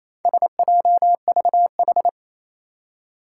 Morse code, heard as a keyed tone, about 29 wpm, 700 Hz. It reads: SJVH
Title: Four Challenging Letter-Number Combinations 28wpm